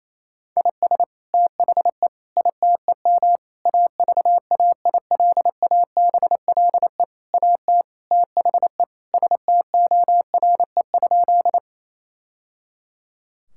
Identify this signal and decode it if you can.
Morse code, heard as a keyed tone, about 28 wpm, 700 Hz. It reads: IS THE ITEM AVAILABLE AT THE STORE?
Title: Sentences from Top 1000 Words - Review Part 6 - 28wpm